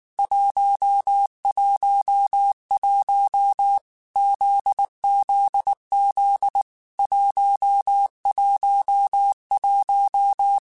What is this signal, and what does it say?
Morse code, heard as a keyed tone, about 19 wpm, 800 Hz. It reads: 111 ZZZ 111
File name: Basic Sets / Basic Letter Groups Collective7 / JZ17@18